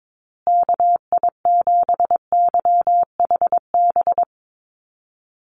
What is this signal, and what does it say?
Morse code, heard as a keyed tone, about 22 wpm, 700 Hz. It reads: KI7YHB
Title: USA Call Signs Repeated 3x 22wpm